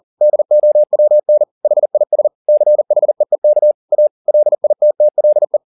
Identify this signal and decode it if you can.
Morse code, heard as a keyed tone, about 40 wpm, 600 Hz. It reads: DOWN HIS CHEEK A LITTLE